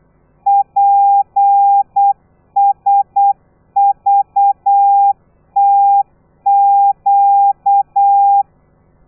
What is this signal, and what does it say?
Morse code, heard as a keyed tone, about 8 wpm, 800 Hz. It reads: PSVTQ
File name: telegrafia / i2tfj / Lez13 35